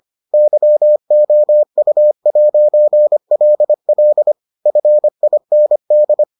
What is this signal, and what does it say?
Morse code, heard as a keyed tone, about 25 wpm, 600 Hz. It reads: YOU'LL FIND